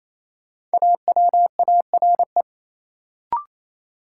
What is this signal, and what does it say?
Morse code, heard as a keyed tone, about 28 wpm, 700 Hz. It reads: AWARE
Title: Sentences from Top 1200 Words - New Word Drill - 28wpm